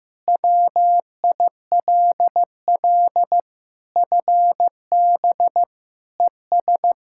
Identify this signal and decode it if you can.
Morse code, heard as a keyed tone, about 16 wpm, 700 Hz. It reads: WILL FB ES
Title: Practice QSO - Exchange 2 - Info - 15wpm